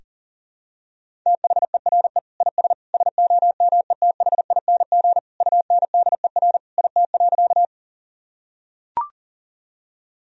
Morse code, heard as a keyed tone, about 40 wpm, 700 Hz. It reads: THERE IS SOMETHING UNDER IT.
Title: Sentences from Top 1000 Words - Review Part 1 - 40wpm